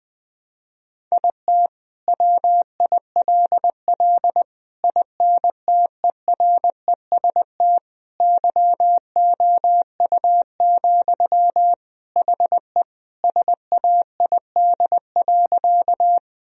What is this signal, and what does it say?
Morse code, heard as a keyed tone, about 21 wpm, 700 Hz. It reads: IT WILL INTEREST YOU, HE SAID.
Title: Sentences from Top 1000 Words - Review Part 2 - 20wpm